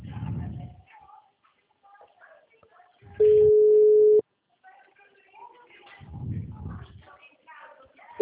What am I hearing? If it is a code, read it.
Ringback tone (Europe, 425 Hz), with speech.
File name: out- -2020- - - .88828